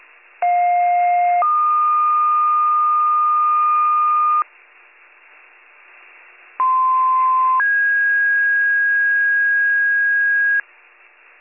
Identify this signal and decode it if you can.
Two-tone page, heard as a one-second tone, then a three-second tone, about 710 then 1220 Hz, about 1040 then 1650 Hz. It is heard from 0.4 to 10.6 s.